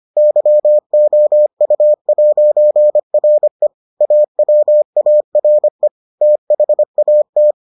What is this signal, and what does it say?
Morse code, heard as a keyed tone, about 25 wpm, 600 Hz. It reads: YOU'RE AWARE THAT